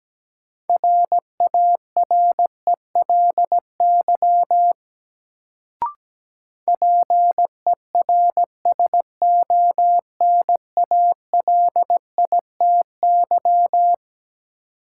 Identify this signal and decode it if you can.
Morse code, heard as a keyed tone, about 17 wpm, 700 Hz. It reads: RARELY PERSONALITY
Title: Sentences from Top 1400 Words - New Word Drill Alt - 17wpm